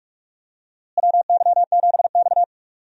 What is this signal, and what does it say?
Morse code, heard as a keyed tone, about 46 wpm, 700 Hz. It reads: WY7X
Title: USA Call Signs - 4 Characters - Repeated 3x - 45wpm